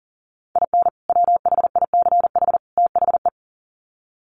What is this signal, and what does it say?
Morse code, heard as a keyed tone, about 40 wpm, 700 Hz. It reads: IN WHICH THE
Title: Sets of 3 Words 40wpm